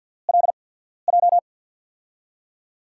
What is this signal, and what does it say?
Morse code, heard as a keyed tone, about 47 wpm, spaced as for 14 wpm, 700 Hz. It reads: LJ
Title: USA Call Signs - 6 Characters - 8x Character Spacing - 50wpm